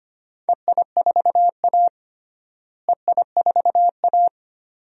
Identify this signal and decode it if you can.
Morse code, heard as a keyed tone, about 25 wpm, 700 Hz. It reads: EI4A EI4A
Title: DXCC-100 Call Signs - 4 Characters - Repeated 2x - 25wpm